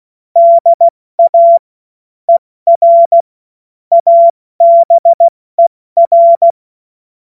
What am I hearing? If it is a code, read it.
Morse code, heard as a keyed tone, about 16 wpm, 680 Hz. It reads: DA ER ABER
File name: Morsekurs / Witze / Witz 07. März 2023 70 F80